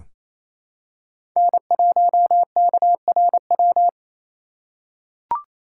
Morse code, heard as a keyed tone, about 28 wpm, 700 Hz. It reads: N1KRW
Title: USA Call Signs - 5 Characters - 2x Character Spacing - 28wpm